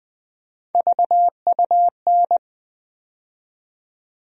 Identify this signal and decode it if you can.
Morse code, heard as a keyed tone, about 20 wpm, 700 Hz. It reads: VUN
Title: USA Call Sign Postfixes - 3 Letters 20wpm